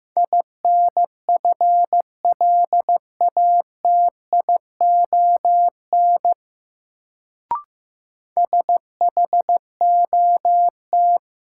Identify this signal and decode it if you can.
Morse code, heard as a keyed tone, about 15 wpm, 700 Hz. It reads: INFLATION SHOT
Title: Sentences from Top 1100 Words - New Word Drill Alt - 15wpm